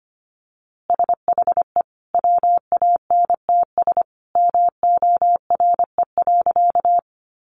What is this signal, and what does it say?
Morse code, heard as a keyed tone, about 25 wpm, 700 Hz. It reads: SHE WANTS MORE.
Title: Easy Phrases 25wpm